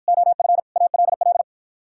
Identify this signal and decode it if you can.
Morse code, heard as a keyed tone, about 52 wpm, 700 Hz. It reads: OF ALL